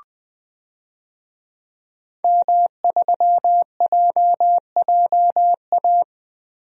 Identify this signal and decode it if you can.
Morse code, heard as a keyed tone, about 20 wpm, 700 Hz. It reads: M3JJA
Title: DXCC-100 Call Signs - 5 Characters - 20wpm